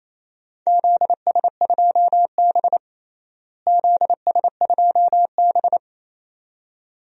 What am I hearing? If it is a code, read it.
Morse code, heard as a keyed tone, about 28 wpm, 700 Hz. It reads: ZS2B ZS2B
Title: DXCC-100 Call Signs - 4 Characters - Repeated 2x - 28wpm